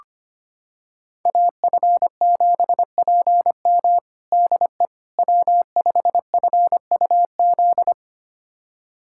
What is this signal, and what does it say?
Morse code, heard as a keyed tone, about 25 wpm, 700 Hz. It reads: AF7PM DE W5FUZ
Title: USA Callsign DE Callsign - 5 Characters - 25wpm